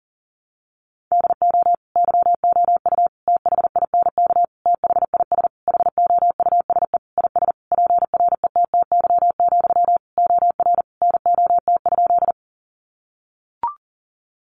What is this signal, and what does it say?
Morse code, heard as a keyed tone, about 40 wpm, 700 Hz. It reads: DO YOU THINK THIS HOUSE IS PRETTY, OR NOT?
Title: Sentences from Top 1000 Words 40wpm